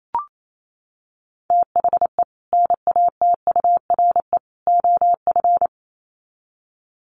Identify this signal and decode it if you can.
Morse code, heard as a keyed tone, about 28 wpm, 700 Hz. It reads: THE NATURE OF
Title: Sets of 3 Words 28wpm